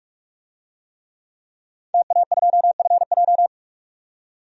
Morse code, heard as a keyed tone, about 42 wpm, 700 Hz. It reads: TA2FJ
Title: DXCC-100 Call Signs - 5 Characters - 45wpm